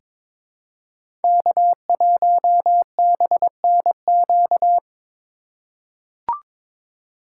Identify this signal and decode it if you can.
Morse code, heard as a keyed tone, about 22 wpm, 700 Hz. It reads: K1BNQ
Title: USA Call Signs - 5 Characters - 4x Character Spacing - 22wpm